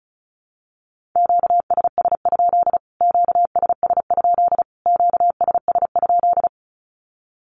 Morse code, heard as a keyed tone, about 34 wpm, 700 Hz. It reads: QSS? QSS? QSS?
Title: Q-Codes Repeated 3x 35wpm